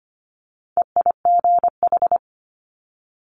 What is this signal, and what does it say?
Morse code, heard as a keyed tone, about 25 wpm, 700 Hz. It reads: EIGH
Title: Multi-letter Phonemes Quick Fire 25wpm